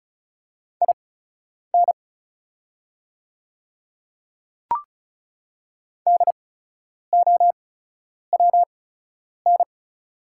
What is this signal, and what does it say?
Morse code, heard as a keyed tone, about 36 wpm, spaced as for 10 wpm, 690 Hz. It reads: IN DOWN
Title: 8x Character Spacing - Top 200 Words - 35wpm